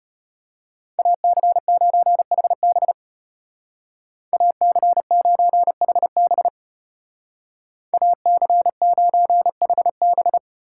Morse code, heard as a keyed tone, about 31 wpm, 700 Hz. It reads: AC9HB AC9HB AC9HB
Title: Speed Racing - USA 2x2 Call Signs 25wpm